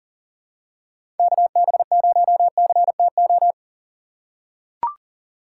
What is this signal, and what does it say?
Morse code, heard as a keyed tone, about 40 wpm, 700 Hz. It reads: KB0CTO
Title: USA Call Signs - 6 Characters - 3x Character Spacing - 40wpm